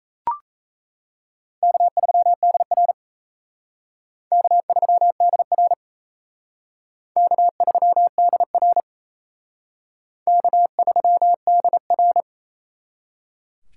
Morse code, heard as a keyed tone, about 35 wpm, 700 Hz. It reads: K3DR K3DR K3DR K3DR